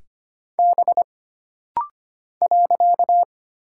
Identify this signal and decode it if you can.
Morse code, heard as a keyed tone, about 25 wpm, 700 Hz. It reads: B .
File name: Single Character (Letter:Number:Punctuation:Prosign) - Rapid-Fire - 25wpm